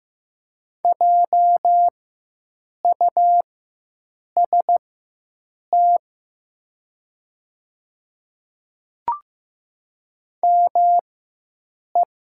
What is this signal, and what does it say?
Morse code, heard as a keyed tone, about 15 wpm, spaced as for 7 wpm, 700 Hz. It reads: JUST ME
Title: 4x Character Spacing - Top 500 Words - 15wpm